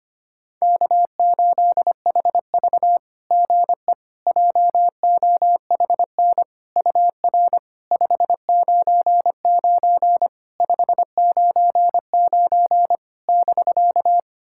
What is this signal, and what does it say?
Morse code, heard as a keyed tone, about 25 wpm, 690 Hz. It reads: K8HV GE JOHN UR 599 599 <BK>
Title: SOTA QSO Practice 25wpm